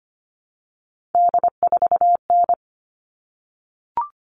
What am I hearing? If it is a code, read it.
Morse code, heard as a keyed tone, about 25 wpm, 700 Hz. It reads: D4N
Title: All Cut Numbers - 3-Digit 25wpm